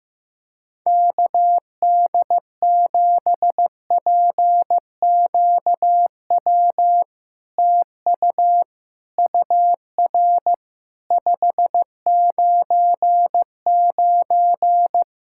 Morse code, heard as a keyed tone, about 16 wpm, 690 Hz. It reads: KD7PQW TU UR 599
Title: SOTA QSO Practice 15wpm